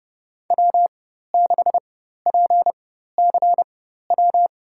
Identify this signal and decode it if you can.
Morse code, heard as a keyed tone, about 30 wpm, spaced as for 14 wpm, 700 Hz. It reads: W6PCW
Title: USA Call Signs - 5 Characters - 4x Character Spacing - 30wpm